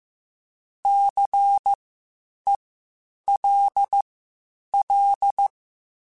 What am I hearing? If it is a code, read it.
Morse code, heard as a keyed tone, about 15 wpm, spaced as for 8 wpm, 800 Hz. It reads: CELL
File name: Spaced Alphas / Spaced Alphas mp3s / Cs sp @13